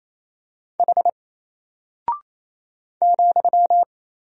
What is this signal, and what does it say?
Morse code, heard as a keyed tone, about 28 wpm, 700 Hz. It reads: H ,
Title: Single Character (Letter:Number:Punctuation:Prosign) Alt 28wpm